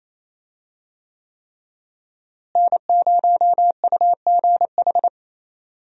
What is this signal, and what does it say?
Morse code, heard as a keyed tone, about 28 wpm, 700 Hz. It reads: N0UGH
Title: USA Call Signs - 5 Characters - Repeated 3x - 28wpm